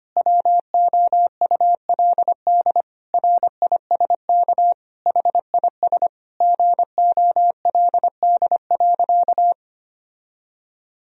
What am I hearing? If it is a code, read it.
Morse code, heard as a keyed tone, about 25 wpm, 700 Hz. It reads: WOULD RISK HIS GOLD.
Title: Sentences from Top 1000 Words - Review Part 4 - 25wpm